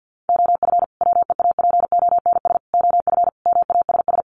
Morse code, heard as a keyed tone, about 50 wpm, 700 Hz. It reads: OF WEAPONS OF MASS
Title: Sets of 5 Words Encore 50wpm